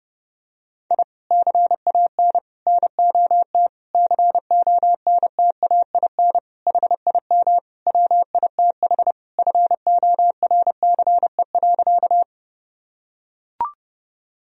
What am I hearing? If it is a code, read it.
Morse code, heard as a keyed tone, about 29 wpm, 700 Hz. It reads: I CAN NOT CONTAIN HIM WITH FORCE.
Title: Sentences from Top 1000 Words - Review Part 3 - 30wpm